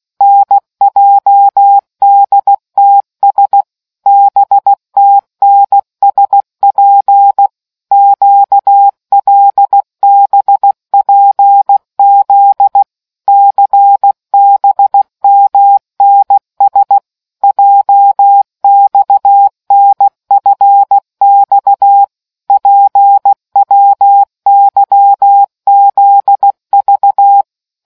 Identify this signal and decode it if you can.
Morse code, heard as a keyed tone, about 16 wpm, 800 Hz. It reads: NJDTS BTNSP QLBPZ CBMNS JXNFX PWYZV